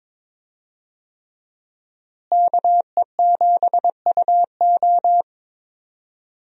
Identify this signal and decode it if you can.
Morse code, heard as a keyed tone, about 22 wpm, 700 Hz. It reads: KE7UO
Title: USA Call Signs - 5 Characters - Repeated 3x - 22wpm